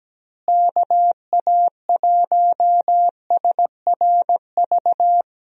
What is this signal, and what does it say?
Morse code, heard as a keyed tone, about 17 wpm, 700 Hz. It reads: KA1SRV
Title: USA Call Signs - 6 Characters - Repeated 3x - 17wpm